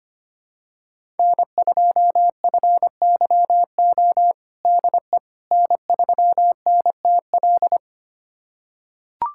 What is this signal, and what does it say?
Morse code, heard as a keyed tone, about 25 wpm, 700 Hz. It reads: N2FYO DE N3NTL
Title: USA Callsign DE Callsign - 5 Characters - 25wpm